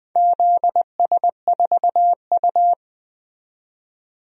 Morse code, heard as a keyed tone, about 20 wpm, 700 Hz. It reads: ZS4U
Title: DXCC-100 Call Signs - 4 Characters - 20wpm